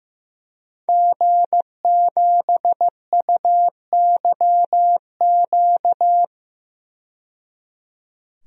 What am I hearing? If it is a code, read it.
Morse code, heard as a keyed tone, about 15 wpm, 700 Hz. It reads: G7UYQ
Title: DXCC-100 Call Signs - 5 Characters - 15wpm